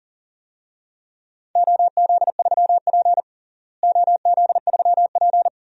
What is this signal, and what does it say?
Morse code, heard as a keyed tone, about 40 wpm, 700 Hz. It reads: OZ3P OZ3P
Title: DXCC-100 Call Signs - 4 Characters - Repeated 3x - 40wpm